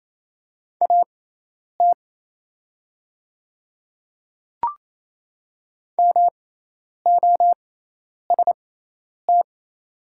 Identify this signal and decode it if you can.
Morse code, heard as a keyed tone, about 28 wpm, spaced as for 10 wpm, 700 Hz. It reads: AT MOST
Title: 6x Character Spacing - Top 300 Words - 28wpm